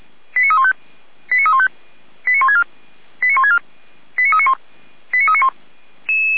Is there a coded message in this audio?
Yes: five-tone selective call (ZVEI) from 0.4 to 5.5 s, heard as quick runs of five tones, tones of about 70 ms. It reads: 89315 89315 89164 89164 89391 89391